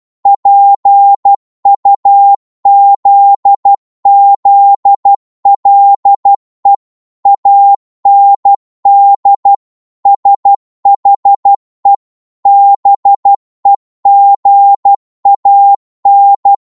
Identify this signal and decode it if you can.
Morse code, heard as a keyed tone, about 12 wpm, 800 Hz. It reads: PUZZLE AND SHE BEGAN